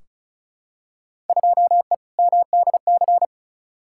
Morse code, heard as a keyed tone, about 36 wpm, 700 Hz. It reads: 2E MDC
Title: Field Day Exchanges - 35wpm